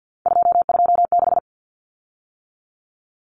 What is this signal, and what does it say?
Morse code, heard as a keyed tone, about 48 wpm, 700 Hz. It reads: 226
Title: Common Cut Numbers - 3-Digit 50wpm